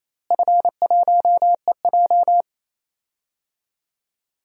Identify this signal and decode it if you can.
Morse code, heard as a keyed tone, about 27 wpm, 700 Hz. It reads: F1EJ
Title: DXCC-100 Call Signs - 4 Characters - Repeated 2x - 28wpm